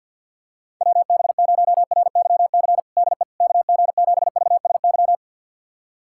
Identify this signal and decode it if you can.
Morse code, heard as a keyed tone, about 52 wpm, 700 Hz. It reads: WB0RYC DE KG7VSY
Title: USA Callsign DE Callsign 50wpm